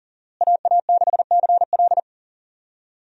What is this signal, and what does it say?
Morse code, heard as a keyed tone, about 40 wpm, 700 Hz. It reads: AA6CL
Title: USA Call Signs - 5 Characters - Repeated 2x - 40wpm